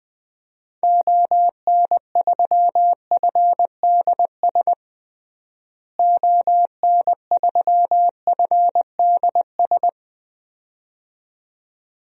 Morse code, heard as a keyed tone, about 21 wpm, 700 Hz. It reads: ON3FDS ON3FDS
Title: DXCC-100 Call Signs - 6 Characters - Repeated 2x - 20wpm